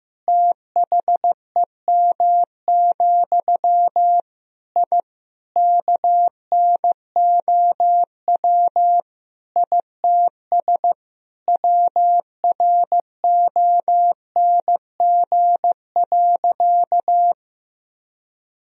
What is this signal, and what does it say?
Morse code, heard as a keyed tone, about 15 wpm, 700 Hz. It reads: THEM, I KNOW ITS WRONG.
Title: Sentences from Top 1000 Words - Review Part 1 - 15wpm